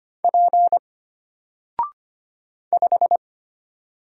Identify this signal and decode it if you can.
Morse code, heard as a keyed tone, about 25 wpm, 700 Hz. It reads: P 5
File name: Single-Letter-Number-25wpm